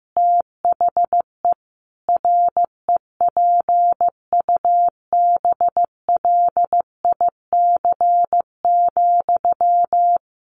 Morse code, heard as a keyed tone, about 15 wpm, 700 Hz. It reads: THE REPUBLIC,
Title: Sentences from Top 1500 Words 15wpm